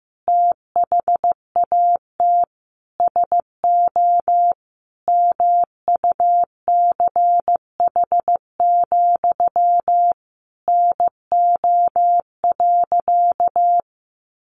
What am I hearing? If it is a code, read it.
Morse code, heard as a keyed tone, about 15 wpm, 700 Hz. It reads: THAT SO MUCH, NO.
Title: A Sound Of Thunder 15wpm